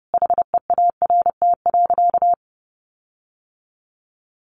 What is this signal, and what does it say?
Morse code, heard as a keyed tone, about 30 wpm, 700 Hz. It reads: HEART.
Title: Sentences from Top 1400 Words 30wpm